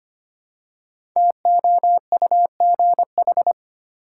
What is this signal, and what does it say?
Morse code, heard as a keyed tone, about 25 wpm, 700 Hz. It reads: TOUGH
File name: 5 Letter Words - First 3 Letters Given - 25wpm